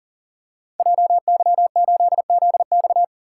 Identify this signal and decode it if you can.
Morse code, heard as a keyed tone, about 42 wpm, 700 Hz. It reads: JY8ZX
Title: Code Groups - 5 Letter:Numbers - 40wpm